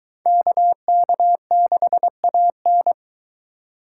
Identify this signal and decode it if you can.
Morse code, heard as a keyed tone, about 20 wpm, 700 Hz. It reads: KK6AN